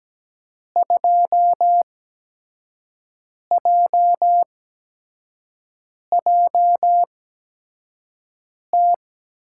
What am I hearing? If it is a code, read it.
Morse code, heard as a keyed tone, about 17 wpm, spaced as for 5 wpm, 700 Hz. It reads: 2JJT
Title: USA Call Signs - 6 Characters - 8x Character Spacing - 17wpm